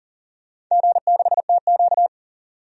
Morse code, heard as a keyed tone, about 40 wpm, 700 Hz. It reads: G6TQ
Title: DXCC-100 Call Signs - 4 Characters - 40wpm